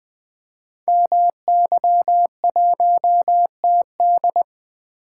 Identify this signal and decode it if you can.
Morse code, heard as a keyed tone, about 20 wpm, 700 Hz. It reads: MY1TD
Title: Code Groups - 5 Letter:Numbers - 20wpm